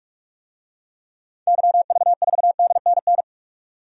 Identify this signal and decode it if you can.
Morse code, heard as a keyed tone, about 47 wpm, 700 Hz. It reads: YV4DNN
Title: DXCC-100 Call Signs - 6 Characters - 45wpm